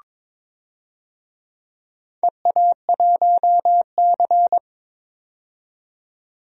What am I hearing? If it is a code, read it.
Morse code, heard as a keyed tone, about 23 wpm, 700 Hz. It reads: EA1C
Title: DXCC-100 Call Signs - 4 Characters - 22wpm